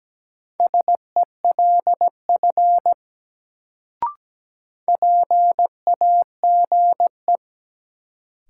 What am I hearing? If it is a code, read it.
Morse code, heard as a keyed tone, about 17 wpm, 700 Hz. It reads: SELF PAGE
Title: Sentences from Top 1000 Words - Review Part 1 - New Word Drill - 17wpm